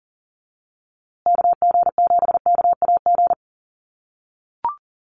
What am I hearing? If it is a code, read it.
Morse code, heard as a keyed tone, about 40 wpm, 700 Hz. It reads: KG7KAG
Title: USA Call Signs Repeated 3x 40wpm